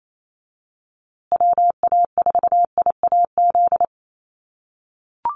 Morse code, heard as a keyed tone, about 28 wpm, 700 Hz. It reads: WA4IAZ
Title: USA Call Signs Repeated 3x 28wpm